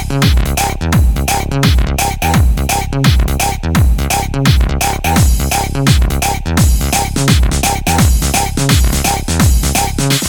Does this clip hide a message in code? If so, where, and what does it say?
No.